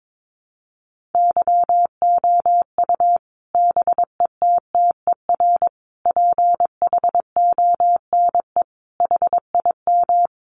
Morse code, heard as a keyed tone, about 22 wpm, 700 Hz. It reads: YOU BETTER PHONE HIM
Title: Sentences from Top 1100 Words 22wpm